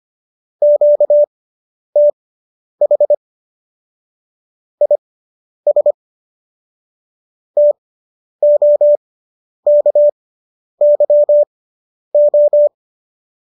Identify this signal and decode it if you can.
Morse code, heard as a keyed tone, about 25 wpm, spaced as for 10 wpm, 590 Hz. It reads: QTH IS TOKYO